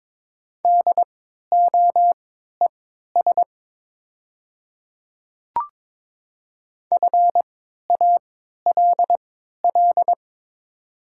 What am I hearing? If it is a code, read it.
Morse code, heard as a keyed tone, about 22 wpm, spaced as for 12 wpm, 700 Hz. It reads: DOES FALL
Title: 3x Character Spacing - Top 500 Words - 22wpm